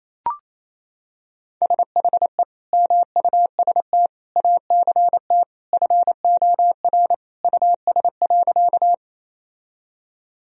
Morse code, heard as a keyed tone, about 28 wpm, 700 Hz. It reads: SHE MUST ACT FOR US.
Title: Sentences from Top 1000 Words - Review Part 1 - 28wpm